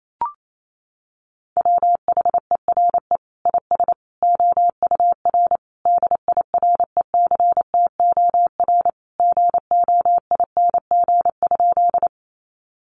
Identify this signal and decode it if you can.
Morse code, heard as a keyed tone, about 28 wpm, 700 Hz. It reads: WHERE IS OUR DIRECTOR GOING?
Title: Sentences from Top 1300 Words 28wpm